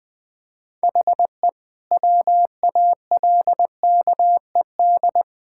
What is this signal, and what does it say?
Morse code, heard as a keyed tone, about 20 wpm, 700 Hz. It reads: HE WALKED
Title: Sentences with 4-Word Phrases - Spoken Beginning - 20wpm